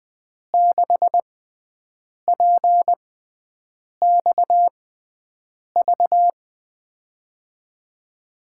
Morse code, heard as a keyed tone, about 20 wpm, spaced as for 7 wpm, 700 Hz. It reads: 6PXV
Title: USA Call Signs - 6 Characters - 6x Character Spacing - 20wpm